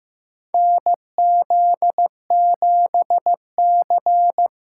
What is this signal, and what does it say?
Morse code, heard as a keyed tone, about 15 wpm, 700 Hz. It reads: NZ7C
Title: USA Callsign DE Callsign - 4 Characters - 15wpm